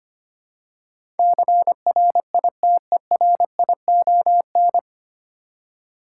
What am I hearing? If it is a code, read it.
Morse code, heard as a keyed tone, about 25 wpm, 700 Hz. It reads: CRITERION
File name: Nine Letter Words - 25wpm